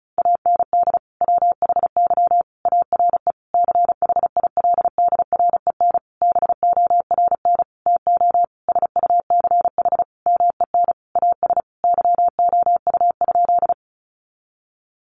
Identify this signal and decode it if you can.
Morse code, heard as a keyed tone, about 34 wpm, 700 Hz. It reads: AND WHY ARE CHILDREN BORN TO SUCH MEN AS YOU?
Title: Sentences from Top 1000 Words 35wpm